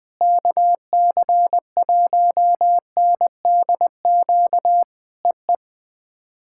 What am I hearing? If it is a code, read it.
Morse code, heard as a keyed tone, about 20 wpm, 700 Hz. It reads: KC1NDQ EE
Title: POTA QSO Practice - 20wpm